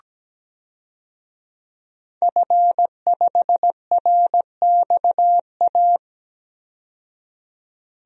Morse code, heard as a keyed tone, about 17 wpm, 700 Hz. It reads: F5RXA
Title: DXCC-100 Call Signs - 5 Characters - 17wpm